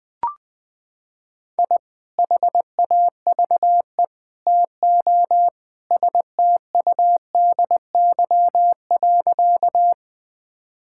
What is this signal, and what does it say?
Morse code, heard as a keyed tone, about 21 wpm, 700 Hz. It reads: I HAVE TO STUDY.
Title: Sentences with 3-Word Phrases - 20wpm